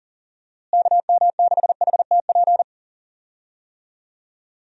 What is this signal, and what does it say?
Morse code, heard as a keyed tone, about 40 wpm, 700 Hz. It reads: KM6HTP
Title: USA Call Signs - 6 Characters - Repeated 2x - 40wpm